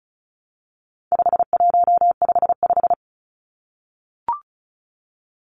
Morse code, heard as a keyed tone, about 34 wpm, 700 Hz. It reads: 5155
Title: Numbers - 4-Digit 35wpm